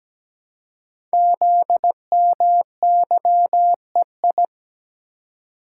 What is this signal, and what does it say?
Morse code, heard as a keyed tone, about 17 wpm, 700 Hz. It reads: ZMYEI
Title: Code Groups - 5 Letters - 17wpm